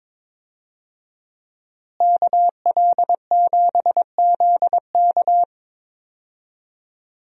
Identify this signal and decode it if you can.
Morse code, heard as a keyed tone, about 21 wpm, 700 Hz. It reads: KL7ZK
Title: USA Call Signs - 5 Characters - Repeated 3x - 22wpm